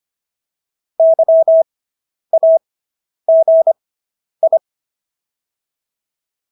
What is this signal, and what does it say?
Morse code, heard as a keyed tone, about 25 wpm, spaced as for 10 wpm, 650 Hz. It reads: YAGI